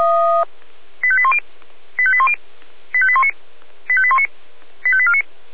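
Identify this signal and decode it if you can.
Five-tone selective call (ZVEI), heard as quick runs of five tones, tones of about 70 ms. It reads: 85810 85810 85819 85819 85849
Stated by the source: ZVEI five-tone selective call received on SDR.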